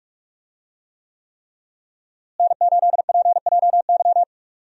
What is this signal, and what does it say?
Morse code, heard as a keyed tone, about 46 wpm, 700 Hz. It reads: N8PJY
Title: USA Call Signs - 5 Characters - Repeated 3x - 45wpm